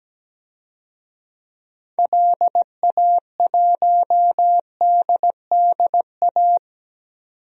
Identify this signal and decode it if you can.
Morse code, heard as a keyed tone, about 17 wpm, 700 Hz. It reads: LA1DDA
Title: DXCC-100 Call Signs - 6 Characters - 17wpm